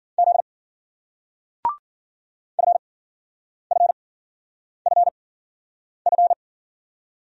Morse code, heard as a keyed tone, about 56 wpm, spaced as for 9 wpm, 700 Hz. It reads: Z FFFF